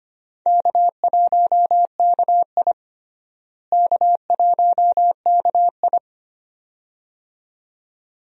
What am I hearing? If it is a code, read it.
Morse code, heard as a keyed tone, about 25 wpm, 700 Hz. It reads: K1KI K1KI
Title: USA Call Signs - 4 Characters - Repeated 2x - 25wpm